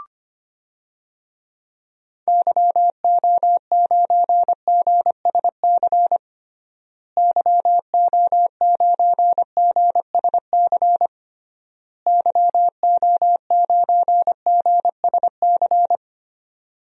Morse code, heard as a keyed tone, about 25 wpm, 700 Hz. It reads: YO9GSC YO9GSC YO9GSC
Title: DXCC-100 Call Signs - 6 Characters - Repeated 3x - 25wpm